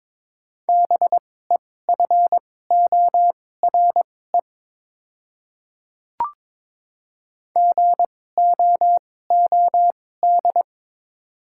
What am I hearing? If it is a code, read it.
Morse code, heard as a keyed tone, about 21 wpm, spaced as for 16 wpm, 700 Hz. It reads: BEFORE GOOD
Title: 2x Character Spacing - Top 200 Words - 22wpm